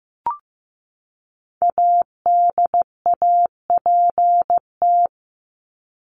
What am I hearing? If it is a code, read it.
Morse code, heard as a keyed tone, about 15 wpm, 700 Hz. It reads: ADAPT
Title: Top 1501-1750 Words 15wpm